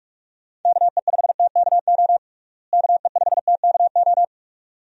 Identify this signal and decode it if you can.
Morse code, heard as a keyed tone, about 42 wpm, 700 Hz. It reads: KE5TKO KE5TKO
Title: USA Call Signs - 6 Characters - Repeated 2x - 45wpm